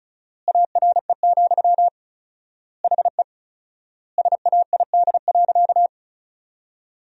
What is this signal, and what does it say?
Morse code, heard as a keyed tone, about 34 wpm, 700 Hz. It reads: ARE, HE SAID.
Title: 4x Word-Spacing - Sentences from Top 100 Words - 35wpm